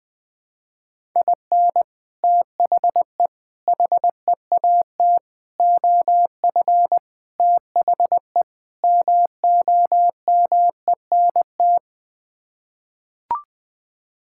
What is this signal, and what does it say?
Morse code, heard as a keyed tone, about 21 wpm, 700 Hz. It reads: IN THE HEAT OF THE MOMENT
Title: Sets of 6 Words - 20wpm